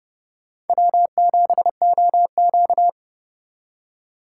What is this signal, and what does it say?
Morse code, heard as a keyed tone, about 30 wpm, 700 Hz. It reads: W7OQ
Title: K1USN SST QSO Practice - 30wpm